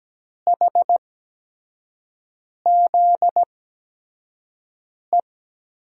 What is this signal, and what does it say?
Morse code, heard as a keyed tone, about 17 wpm, spaced as for 5 wpm, 700 Hz. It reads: HZE
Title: USA Call Signs - 6 Characters - 8x Character Spacing - 17wpm